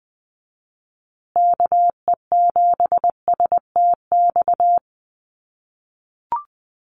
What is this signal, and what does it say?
Morse code, heard as a keyed tone, about 20 wpm, 700 Hz. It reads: KE7STX
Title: USA Call Signs Repeated 2x 20wpm